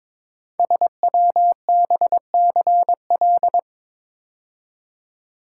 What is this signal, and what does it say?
Morse code, heard as a keyed tone, about 22 wpm, 700 Hz. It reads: SWBCL
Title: Code Groups - 5 Letters - 22wpm